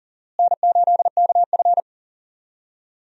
Morse code, heard as a keyed tone, about 40 wpm, 700 Hz. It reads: N8KF
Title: USA Call Signs - 4 Characters - Repeated 2x - 40wpm